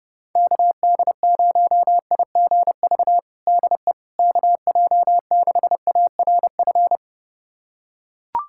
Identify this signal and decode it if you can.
Morse code, heard as a keyed tone, about 32 wpm, 700 Hz. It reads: KD0IGV DE KJ6ARF
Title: USA Callsign DE Callsign - 6 Characters - 30wpm